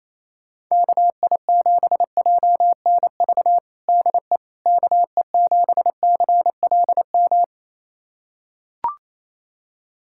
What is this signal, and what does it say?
Morse code, heard as a keyed tone, about 27 wpm, 700 Hz. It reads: KI7JNV DE KE7CLM
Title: USA Callsign DE Callsign - 6 Characters - 28wpm